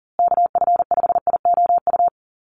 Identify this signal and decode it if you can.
Morse code, heard as a keyed tone, about 40 wpm, 700 Hz. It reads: KF5IOU
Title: USA Call Signs Repeated 2x 40wpm